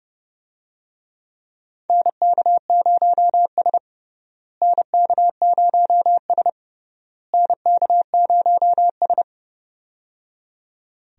Morse code, heard as a keyed tone, about 30 wpm, 700 Hz. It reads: NK0S NK0S NK0S
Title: USA Call Signs - 4 Characters - Repeated 3x - 30wpm